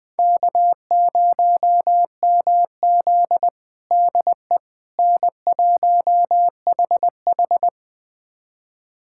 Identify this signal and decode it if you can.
Morse code, heard as a keyed tone, about 20 wpm, 700 Hz. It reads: K0MZ DE N1HH
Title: USA Callsign DE Callsign - 4 Characters - 20wpm